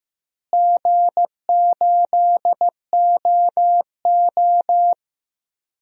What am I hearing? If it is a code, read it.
Morse code, heard as a keyed tone, about 15 wpm, 700 Hz. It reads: G8OO
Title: DXCC-100 Call Signs - 4 Characters - Repeated 2x - 15wpm